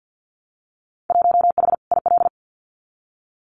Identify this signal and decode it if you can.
Morse code, heard as a keyed tone, about 50 wpm, 700 Hz. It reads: 1H IL
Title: Winter Field Day Exchanges 50wpm